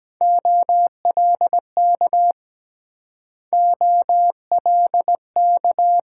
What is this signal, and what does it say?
Morse code, heard as a keyed tone, about 23 wpm, 690 Hz. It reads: OLK OLK